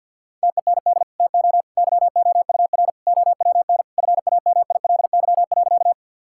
Morse code, heard as a keyed tone, about 50 wpm, 700 Hz. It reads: TEND TO YOUR OWN FAMILY.
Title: Sentences from Top 1000 Words - Review Part 5 - 50wpm